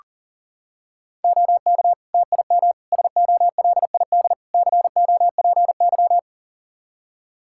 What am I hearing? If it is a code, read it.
Morse code, heard as a keyed tone, about 40 wpm, 700 Hz. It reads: OK TIM SOLID COPY
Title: Practice QSO - Exchange 3 - More Info - 40wpm